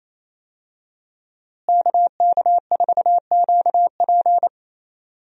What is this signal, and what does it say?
Morse code, heard as a keyed tone, about 28 wpm, 700 Hz. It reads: KK4QP
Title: USA Call Signs - 5 Characters - Repeated 3x - 28wpm